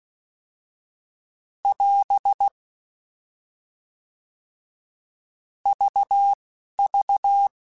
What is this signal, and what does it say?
Morse code, heard as a keyed tone, about 16 wpm, 800 Hz. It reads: <AS> V V